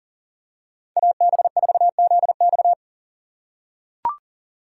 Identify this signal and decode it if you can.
Morse code, heard as a keyed tone, about 40 wpm, 700 Hz. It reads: AB4ZX
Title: USA Call Signs - 5 Characters - 6x Character Spacing - 40wpm